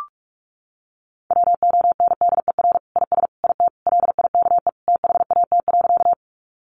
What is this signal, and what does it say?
Morse code, heard as a keyed tone, about 44 wpm, 700 Hz. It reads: WONDER IS IT LIKE THAT.
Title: Sentences from Top 500 Words 45wpm